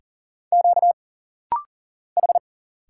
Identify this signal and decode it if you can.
Morse code, heard as a keyed tone, about 40 wpm, 700 Hz. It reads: Q H
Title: Single Letter-Number Alt - Rapid-Fire 40wpm